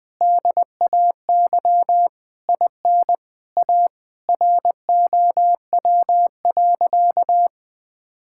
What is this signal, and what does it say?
Morse code, heard as a keyed tone, about 20 wpm, 700 Hz. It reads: DAY IN A ROW.
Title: Sentences with 7-Word Phrases - Spoken Beginning - 20wpm